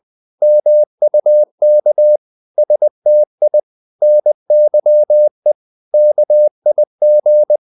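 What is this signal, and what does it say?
Morse code, heard as a keyed tone, about 20 wpm, 600 Hz. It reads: MUK STI NYE KIG
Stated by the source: Morse code practice of short Danish words at 20 wpm, from CW med Gnister.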